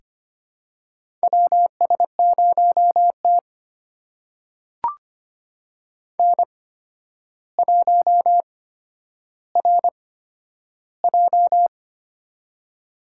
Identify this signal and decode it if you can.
Morse code, heard as a keyed tone, about 25 wpm, spaced as for 11 wpm, 700 Hz. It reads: WS0T N1RJ
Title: USA Call Signs - 4 Characters - 8x Character Spacing - 25wpm